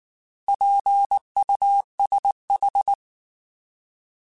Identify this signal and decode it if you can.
Morse code, heard as a keyed tone, about 19 wpm, 800 Hz. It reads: PUSH
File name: Alphabetical / Alphabetical mp3s / Ps @18